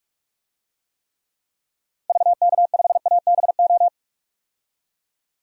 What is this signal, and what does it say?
Morse code, heard as a keyed tone, about 46 wpm, 700 Hz. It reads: VK5ABO
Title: DXCC-100 Call Signs - 6 Characters - 45wpm